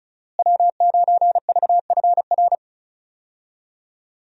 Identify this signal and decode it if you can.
Morse code, heard as a keyed tone, about 36 wpm, 700 Hz. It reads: W9VFR
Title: USA Call Signs - 5 Characters - Repeated 3x - 35wpm